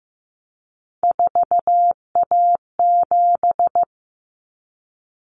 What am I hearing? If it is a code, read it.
Morse code, heard as a keyed tone, about 15 wpm, 700 Hz. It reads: 4A7
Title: Common Cut Numbers - 3-Digit 15wpm